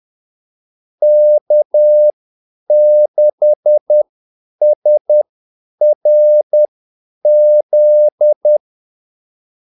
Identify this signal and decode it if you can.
Morse code, heard as a keyed tone, about 10 wpm, spaced as for 8 wpm, 600 Hz. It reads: K6SRZ
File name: Callsigns-10-8wpm